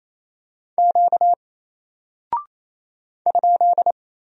Morse code, heard as a keyed tone, about 27 wpm, 700 Hz. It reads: Q ?
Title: Single Character (Letter:Number:Punctuation:Prosign) 28wpm